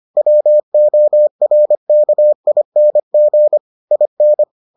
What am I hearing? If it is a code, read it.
Morse code, heard as a keyed tone, about 25 wpm, 600 Hz. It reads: WORKING IN